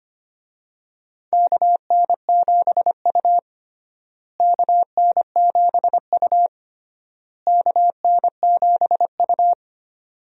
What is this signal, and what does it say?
Morse code, heard as a keyed tone, about 25 wpm, 700 Hz. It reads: KN7U KN7U KN7U
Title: USA Call Signs - 4 Characters - Repeated 3x - 25wpm